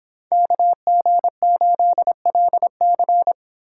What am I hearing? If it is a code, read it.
Morse code, heard as a keyed tone, about 27 wpm, 700 Hz. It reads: KG8LC
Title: Speed Racing - USA 2x2 Call Signs 22wpm